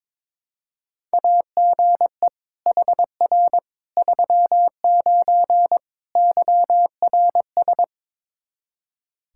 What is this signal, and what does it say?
Morse code, heard as a keyed tone, about 22 wpm, 700 Hz. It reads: AGE HR 39 YRS
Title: Practice QSO - Exchange 3 - More Info - 22wpm